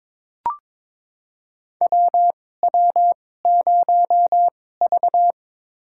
Morse code, heard as a keyed tone, about 22 wpm, spaced as for 16 wpm, 700 Hz. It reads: WW0V
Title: USA Call Signs - 2x Character Spacing - 22wpm